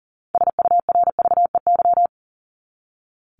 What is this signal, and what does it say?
Morse code, heard as a keyed tone, about 40 wpm, 700 Hz. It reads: SURVEY
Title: Top 1501-1750 Words 40wpm